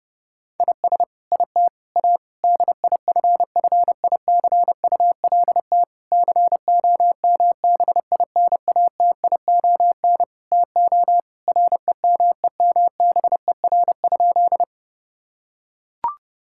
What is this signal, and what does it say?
Morse code, heard as a keyed tone, about 30 wpm, 700 Hz. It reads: IS IT A DIFFICULT COMBINATION TO REMEMBER?
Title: Sentences from Top 1000 Words - Review Part 6 - 30wpm